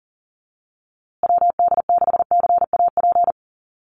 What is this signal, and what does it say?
Morse code, heard as a keyed tone, about 40 wpm, 700 Hz. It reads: WD6CAP
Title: USA Call Signs Repeated 2x 40wpm